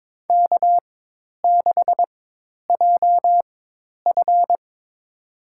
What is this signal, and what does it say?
Morse code, heard as a keyed tone, about 22 wpm, spaced as for 10 wpm, 700 Hz. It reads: K6JF
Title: USA Call Signs - 4 Characters - 4x Character Spacing - 22wpm